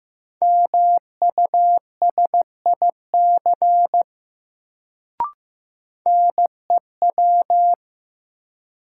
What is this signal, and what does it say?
Morse code, heard as a keyed tone, about 16 wpm, 700 Hz. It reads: MUSIC NEW
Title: Sentences from Top 300 Words - New Word Drill - 15wpm